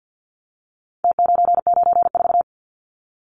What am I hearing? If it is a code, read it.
Morse code, heard as a keyed tone, about 50 wpm, 700 Hz. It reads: T994
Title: Numbers - Contest Style - 4-Digit 50wpm